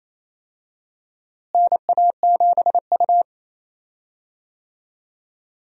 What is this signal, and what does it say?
Morse code, heard as a keyed tone, about 28 wpm, 700 Hz. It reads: NA7U
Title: USA Call Signs - 4 Characters - Repeated 2x - 28wpm